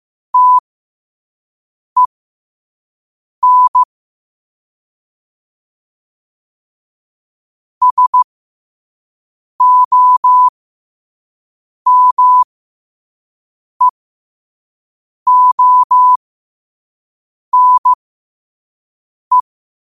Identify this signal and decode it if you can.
Morse code, heard as a keyed tone, about 15 wpm, spaced as for 5 wpm, 1000 Hz. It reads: TEN SOMEONE
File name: qotdfiles / 5wpm